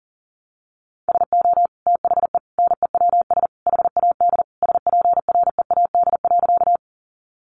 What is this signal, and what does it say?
Morse code, heard as a keyed tone, about 40 wpm, 700 Hz. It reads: SO THE NEWS HAD SPREAD.
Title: Sentences from Top 1200 Words 40wpm